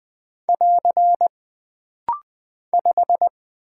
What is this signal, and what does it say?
Morse code, heard as a keyed tone, about 20 wpm, 700 Hz. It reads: <AR> 5
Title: Single Character (Letter:Number:Punctuation:Prosign) - Rapid-Fire - 20wpm